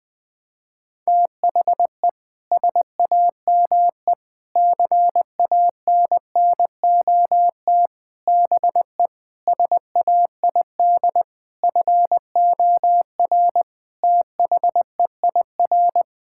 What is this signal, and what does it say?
Morse code, heard as a keyed tone, about 20 wpm, 700 Hz. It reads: THE SAME CANNOT BE SAID FOR THEIR
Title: Sentences with 7-Word Phrases - 20wpm